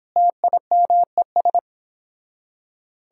Morse code, heard as a keyed tone, about 26 wpm, 700 Hz. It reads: TIMES